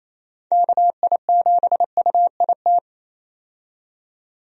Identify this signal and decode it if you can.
Morse code, heard as a keyed tone, about 28 wpm, 700 Hz. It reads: KI7UIT
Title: USA Call Signs - 6 Characters - 28wpm